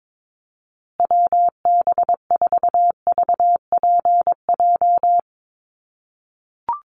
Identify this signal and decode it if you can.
Morse code, heard as a keyed tone, about 22 wpm, 700 Hz. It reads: WB4VPJ
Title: USA Call Signs Repeated 2x 22wpm